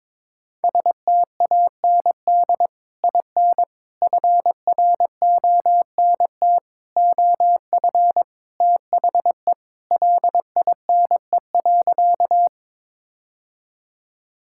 Morse code, heard as a keyed tone, about 22 wpm, 700 Hz. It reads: STAND IN FRONT OF THE LINE.
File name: practice / 22wpm / Sentences with 3-Word Phrases - 22wpm